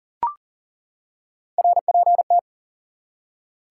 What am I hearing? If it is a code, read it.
Morse code, heard as a keyed tone, about 40 wpm, 700 Hz. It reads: RPT
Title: Top 180 QSO Elements 40wpm